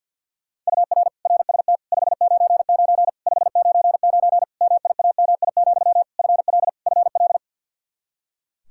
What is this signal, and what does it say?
Morse code, heard as a keyed tone, about 50 wpm, 700 Hz. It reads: UR RST 599 599 MIAMI, FL FL
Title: Practice QSO - Exchange 1 - Beginning - 50wpm